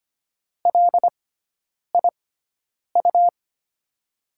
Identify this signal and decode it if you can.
Morse code, heard as a keyed tone, about 25 wpm, spaced as for 9 wpm, 700 Hz. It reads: LIU
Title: USA Call Signs - 6x Character Spacing - 25wpm